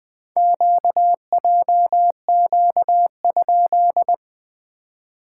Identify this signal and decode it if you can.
Morse code, heard as a keyed tone, about 20 wpm, 700 Hz. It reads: QJQ?
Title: SOTA QSO Practice 20wpm